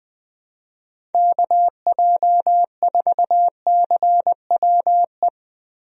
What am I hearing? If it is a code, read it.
Morse code, heard as a keyed tone, about 20 wpm, 700 Hz. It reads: KJ4CWE
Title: USA Call Signs - 6 Characters - Repeated 2x - 20wpm